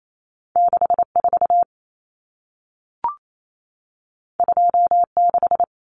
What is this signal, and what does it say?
Morse code, heard as a keyed tone, about 28 wpm, 700 Hz. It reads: 64 26
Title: Common Cut Numbers - 2-Digit 28wpm